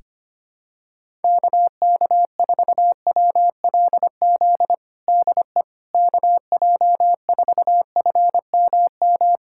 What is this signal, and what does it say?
Morse code, heard as a keyed tone, about 25 wpm, 700 Hz. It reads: KK4WLZ DE KJ4FMM
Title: USA Callsign DE Callsign - 6 Characters - 25wpm